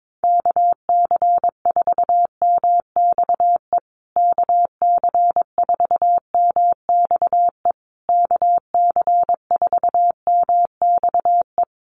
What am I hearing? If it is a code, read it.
Morse code, heard as a keyed tone, about 22 wpm, 700 Hz. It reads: KC4MXE KC4MXE KC4MXE
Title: USA Call Signs Repeated 3x 22wpm